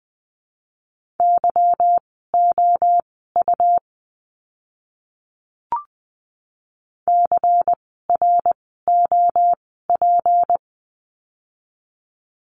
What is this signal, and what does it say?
Morse code, heard as a keyed tone, about 20 wpm, spaced as for 14 wpm, 700 Hz. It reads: YOU CROP
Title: 153 Lesson 25f - Letter p - Words 20wpm